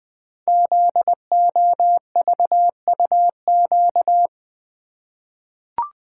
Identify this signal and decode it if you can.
Morse code, heard as a keyed tone, about 20 wpm, 700 Hz. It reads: ZOVUQ
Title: Code Groups - 5 Letters - 20wpm